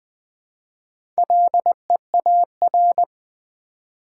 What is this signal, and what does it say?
Morse code, heard as a keyed tone, about 20 wpm, 700 Hz. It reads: LEAR
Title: Build A Word - 1 Letter At A Time - 5-Letter Words - 20wpm